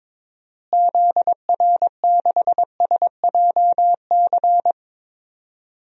Morse code, heard as a keyed tone, about 22 wpm, 700 Hz. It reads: ZR6SJC
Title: DXCC-100 Call Signs - 6 Characters - Repeated 3x - 22wpm